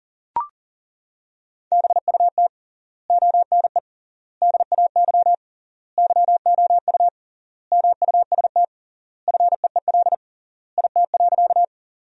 Morse code, heard as a keyed tone, about 40 wpm, 700 Hz. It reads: BUT ONE DAY YOU MUST FEEL IT.
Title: 3x Word-Spacing - Sentences from Top 400 Words - 40wpm